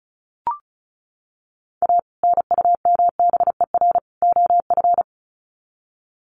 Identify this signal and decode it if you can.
Morse code, heard as a keyed tone, about 34 wpm, 700 Hz. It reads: A NUMBER OF
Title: Sets of 3 Words 35wpm